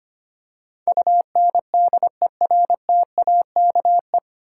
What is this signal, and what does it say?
Morse code, heard as a keyed tone, about 25 wpm, 700 Hz. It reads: UNDERTAKE
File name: Nine Letter Words - 25wpm